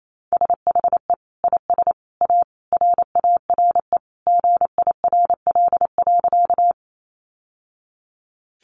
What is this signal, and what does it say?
Morse code, heard as a keyed tone, about 28 wpm, 700 Hz. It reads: SHE IS A RARE GIRL.
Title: Sentences from Top 1100 Words 28wpm